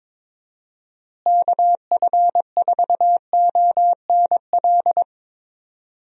Morse code, heard as a keyed tone, about 21 wpm, 700 Hz. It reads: KF4ONL
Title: USA Call Signs - 6 Characters - Repeated 2x - 22wpm